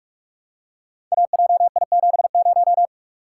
Morse code, heard as a keyed tone, about 44 wpm, 700 Hz. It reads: AJI70
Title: Code Groups - 5 Letter:Numbers - 45wpm